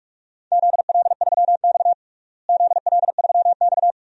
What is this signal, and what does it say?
Morse code, heard as a keyed tone, about 46 wpm, 700 Hz. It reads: ZL3X ZL3X
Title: DXCC-100 Call Signs - 4 Characters - Repeated 2x - 45wpm